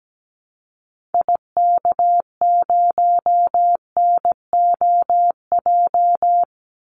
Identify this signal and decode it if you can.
Morse code, heard as a keyed tone, about 17 wpm, 700 Hz. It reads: IK0NOJ
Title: Call Signs Repeated 2x - 17wpm